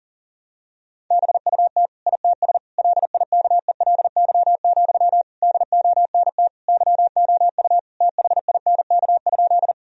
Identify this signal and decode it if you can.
Morse code, heard as a keyed tone, about 40 wpm, 700 Hz. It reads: BUT ITS LIKELY, DONT YOU THINK?
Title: Sentences from Top 1000 Words - Review Part 4 - 40wpm